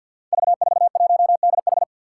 Morse code, heard as a keyed tone, about 49 wpm, 700 Hz. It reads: V41DH
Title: Five Challenging Letter-Number Combinations 50wpm